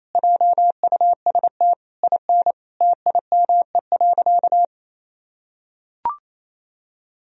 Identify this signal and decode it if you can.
Morse code, heard as a keyed tone, about 28 wpm, 700 Hz. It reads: JUST IN TIME.
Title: Sentences from Top 1000 Words - Review Part 2 - 28wpm